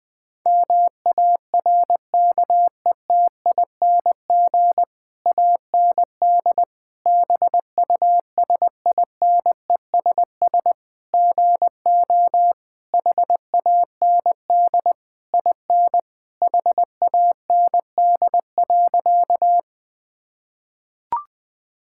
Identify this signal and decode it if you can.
Morse code, heard as a keyed tone, about 20 wpm, 700 Hz. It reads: MARKETING AND BUSINESS GO HAND IN HAND.
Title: Sentences from Top 1000 Words - Review Part 5 - 20wpm